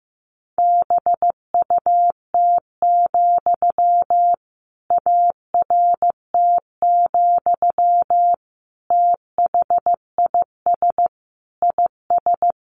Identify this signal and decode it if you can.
Morse code, heard as a keyed tone, about 15 wpm, 700 Hz. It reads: BUT, ART, THIS IS
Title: Sentences from Top 600 Words 15wpm